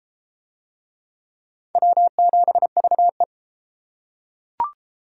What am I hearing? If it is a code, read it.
Morse code, heard as a keyed tone, about 31 wpm, 700 Hz. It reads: W7VE